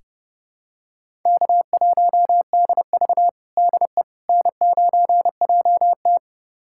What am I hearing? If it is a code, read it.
Morse code, heard as a keyed tone, about 31 wpm, 700 Hz. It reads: K1DV DE N9JT
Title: USA Callsign DE Callsign - 4 Characters - 30wpm